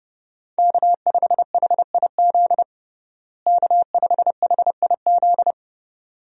Morse code, heard as a keyed tone, about 32 wpm, 700 Hz. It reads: K5HIZ K5HIZ
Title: USA Call Signs - 5 Characters - Repeated 2x - 30wpm